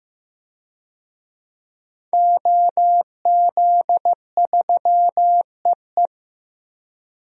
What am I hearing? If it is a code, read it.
Morse code, heard as a keyed tone, about 15 wpm, 700 Hz. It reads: OZ3EE
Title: DXCC-100 Call Signs - 5 Characters - 15wpm